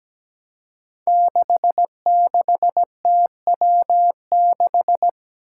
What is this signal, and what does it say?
Morse code, heard as a keyed tone, about 17 wpm, 700 Hz. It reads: 66TW6
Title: Five Challenging Letter-Number Combinations 17wpm